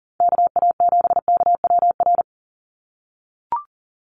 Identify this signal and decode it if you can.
Morse code, heard as a keyed tone, about 40 wpm, 690 Hz. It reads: KA7KWR
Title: USA Call Signs 40wpm